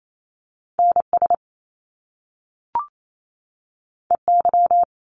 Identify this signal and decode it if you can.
Morse code, heard as a keyed tone, about 28 wpm, 700 Hz. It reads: NS EY
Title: Top Two Letter Combinations in Words 28wpm